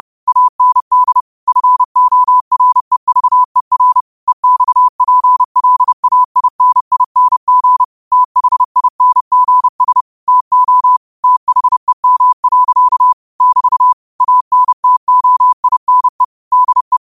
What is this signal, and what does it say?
Morse code, heard as a keyed tone, about 30 wpm, 1000 Hz. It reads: AND FOREVER EXPLAINING THINGS TO THEM. <BT> ANTOINE DE
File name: qotdfiles / 30wpm